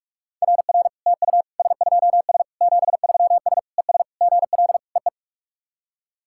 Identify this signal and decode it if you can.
Morse code, heard as a keyed tone, about 46 wpm, 690 Hz. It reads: RR TU S2S 73S ES GL EE
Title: SOTA QSO Practice 45wpm